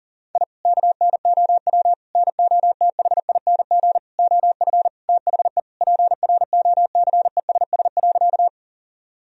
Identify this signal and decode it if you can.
Morse code, heard as a keyed tone, about 40 wpm, 700 Hz. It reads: I KNOW NOTHING OF THE PROCESS.
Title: Sentences from Top 1000 Words - Review Part 3 - 40wpm